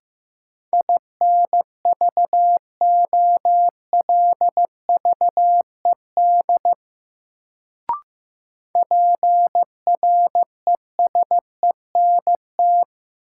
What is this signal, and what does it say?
Morse code, heard as a keyed tone, about 15 wpm, 700 Hz. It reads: INVOLVED PRESENT
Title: Sentences from Top 800 Words - New Word Drill - 15wpm